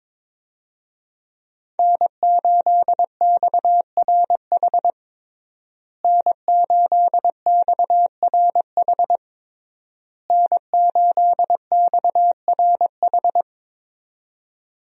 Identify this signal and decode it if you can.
Morse code, heard as a keyed tone, about 22 wpm, 700 Hz. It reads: N8XRH N8XRH N8XRH
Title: USA Call Signs - 5 Characters - Repeated 3x - 22wpm